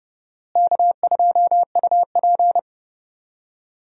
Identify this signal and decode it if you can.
Morse code, heard as a keyed tone, about 30 wpm, 700 Hz. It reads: K2UP
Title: USA Call Signs - 4 Characters - Repeated 3x - 30wpm